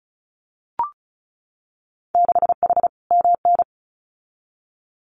Morse code, heard as a keyed tone, about 36 wpm, 700 Hz. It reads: 6H MN
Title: Winter Field Day Exchanges Encore 35wpm